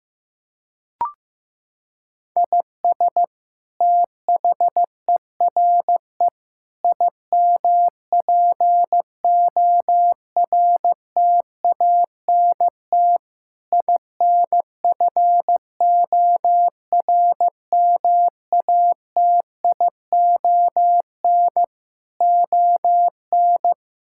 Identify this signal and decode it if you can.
Morse code, heard as a keyed tone, about 15 wpm, 700 Hz. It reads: IS THERE IMPORTANT INFORMATION ON
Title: Sentences from Top 1000 Words - Review Part 6 - 15wpm